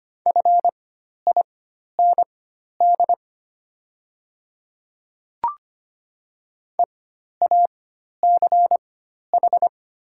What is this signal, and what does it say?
Morse code, heard as a keyed tone, about 25 wpm, spaced as for 12 wpm, 700 Hz. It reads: FIND EACH
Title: 4x Character Spacing - Top 100 Words - 25wpm